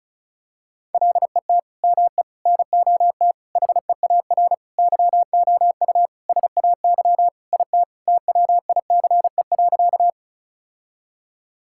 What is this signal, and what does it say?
Morse code, heard as a keyed tone, about 36 wpm, 700 Hz. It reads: LET ME NOT HEAR YOU SAY IT TWICE.
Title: Sentences from Top 1600 Words 35wpm